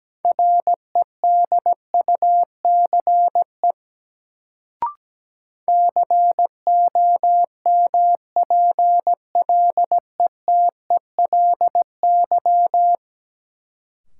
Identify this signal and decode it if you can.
Morse code, heard as a keyed tone, about 17 wpm, 700 Hz. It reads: REDUCE COMPLETELY
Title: Sentences from Top 1100 Words - New Word Drill - 17wpm